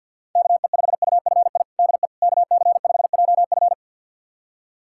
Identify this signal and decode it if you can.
Morse code, heard as a keyed tone, about 51 wpm, 700 Hz. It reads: KE5UFI DE KC5JF
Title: USA Callsign DE Callsign 50wpm